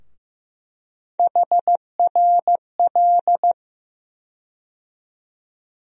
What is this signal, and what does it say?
Morse code, heard as a keyed tone, about 15 wpm, 700 Hz. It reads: HRL
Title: USA Call Sign Postfixes - 3 Letters 15wpm